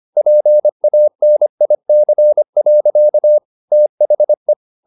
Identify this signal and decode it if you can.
Morse code, heard as a keyed tone, about 25 wpm, 600 Hz. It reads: PANIC. THE
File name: the old man and the sea part 1 25wpm